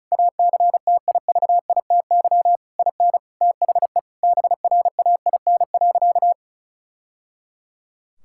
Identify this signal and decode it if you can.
Morse code, heard as a keyed tone, about 36 wpm, 700 Hz. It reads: ACTIVITY IN THE BRAIN.
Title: Sentences from Top 1000 Words - Review Part 6 - 35wpm